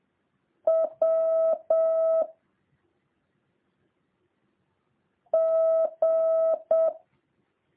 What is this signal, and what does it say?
Morse code, heard as a keyed tone, about 7 wpm, 650 Hz. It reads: W G